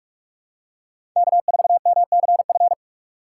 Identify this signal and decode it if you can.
Morse code, heard as a keyed tone, about 47 wpm, 700 Hz. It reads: K4MCF
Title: USA Call Signs - 5 Characters - 2x Character Spacing - 45wpm